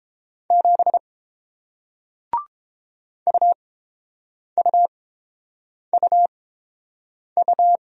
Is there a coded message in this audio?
Yes: Morse code from 0.5 to 7.8 s, heard as a keyed tone, about 30 wpm, spaced as for 7 wpm, 700 Hz. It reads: 7 UUUU